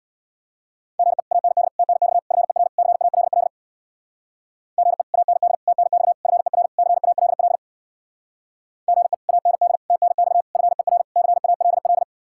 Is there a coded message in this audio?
Yes: Morse code from 1.0 to 12.0 s, heard as a keyed tone, about 64 wpm, 700 Hz. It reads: GE AND TNX FER CALL GE AND TNX FER CALL GE AND TNX FER CALL